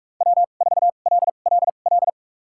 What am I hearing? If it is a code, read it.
Morse code, heard as a keyed tone, about 46 wpm, spaced as for 32 wpm, 700 Hz. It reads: W4LLL
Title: USA Call Signs - 5 Characters - 2x Character Spacing - 45wpm